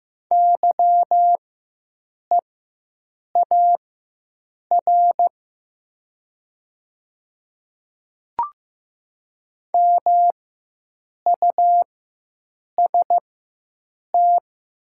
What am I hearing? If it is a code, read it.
Morse code, heard as a keyed tone, about 15 wpm, spaced as for 7 wpm, 690 Hz. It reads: YEAR MUST
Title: 4x Character Spacing - Top 200 Words - 15wpm